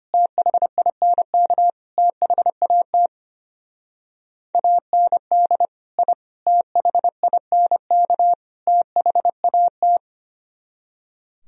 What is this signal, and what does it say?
Morse code, heard as a keyed tone, about 30 wpm, 700 Hz. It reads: THINK THAT AND I THINK THAT
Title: Speed-Racing - Sets of 4 Words 25wpm